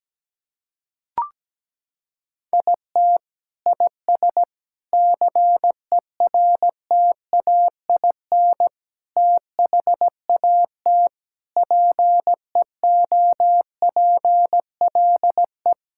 Morse code, heard as a keyed tone, about 17 wpm, 700 Hz. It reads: IT IS CERTAIN THAT PEOPLE
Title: Sentences from Top 1000 Words - Review Part 5 - 17wpm